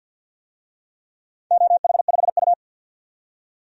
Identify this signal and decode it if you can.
Morse code, heard as a keyed tone, about 51 wpm, 700 Hz. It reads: OH5U
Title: DXCC-100 Call Signs - 4 Characters - 50wpm